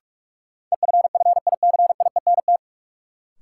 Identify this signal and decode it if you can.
Morse code, heard as a keyed tone, about 42 wpm, 700 Hz. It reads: EFFICIENT
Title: Sentences from Top 1300 Words - New Word Drill - 45wpm